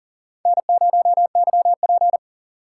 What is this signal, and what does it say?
Morse code, heard as a keyed tone, about 40 wpm, 700 Hz. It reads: N0YP
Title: USA Call Signs - 4 Characters - 2x Character Spacing - 40wpm